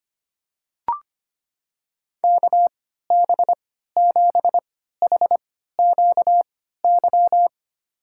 Morse code, heard as a keyed tone, about 25 wpm, spaced as for 14 wpm, 700 Hz. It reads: KB7HQY
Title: USA Call Signs - 6 Characters - 3x Character Spacing - 25wpm